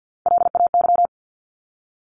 Morse code, heard as a keyed tone, about 50 wpm, 700 Hz. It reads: LAY
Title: Top 500 Words Repeated 50wpm